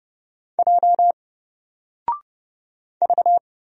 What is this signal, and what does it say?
Morse code, heard as a keyed tone, about 30 wpm, 700 Hz. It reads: J V
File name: Single-Letter-Number-30wpm